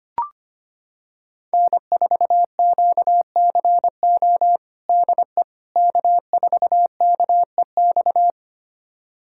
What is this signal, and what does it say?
Morse code, heard as a keyed tone, about 25 wpm, 700 Hz. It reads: N4QCO DE K4KEX
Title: USA Callsign DE Callsign 25wpm